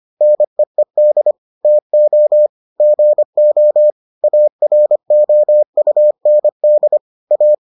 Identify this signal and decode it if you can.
Morse code, heard as a keyed tone, about 25 wpm, 600 Hz. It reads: NEED TO GO AROUND A